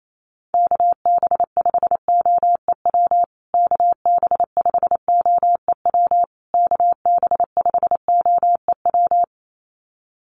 Morse code, heard as a keyed tone, about 28 wpm, 700 Hz. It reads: KB5OEW KB5OEW KB5OEW
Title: USA Call Signs Repeated 3x 28wpm